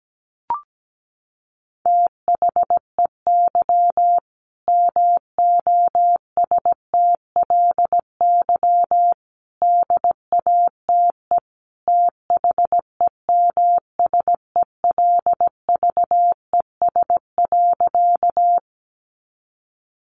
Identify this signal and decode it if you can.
Morse code, heard as a keyed tone, about 17 wpm, 700 Hz. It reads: THEY MOSTLY DATE THEMSELVES.
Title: Sentences from Top 1300 Words 17wpm